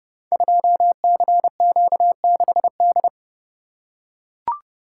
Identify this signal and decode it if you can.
Morse code, heard as a keyed tone, about 30 wpm, 700 Hz. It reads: 2CQ6D
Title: Code Groups - 5 Letter:Numbers - 30wpm